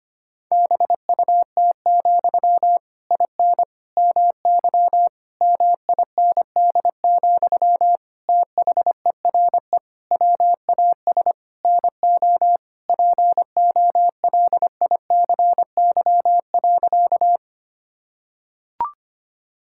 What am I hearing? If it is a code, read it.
Morse code, heard as a keyed tone, about 25 wpm, 700 Hz. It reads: BUT, IN MY MIND, THERE WAS NO POLICY.
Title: Sentences from Top 1000 Words - Review Part 5 - 25wpm